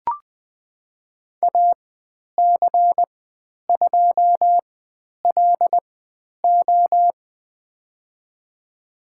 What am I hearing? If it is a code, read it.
Morse code, heard as a keyed tone, about 20 wpm, spaced as for 10 wpm, 700 Hz. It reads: AC2LO
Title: 102 Lesson 17g - Letter u - Call Signs 20wpm